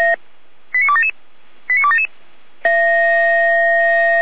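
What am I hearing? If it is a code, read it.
Five-tone selective call (ZVEI), heard as quick runs of five tones, tones of about 80 ms. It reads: 89277 89277
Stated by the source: ZVEI five-tone selective call received by SDR.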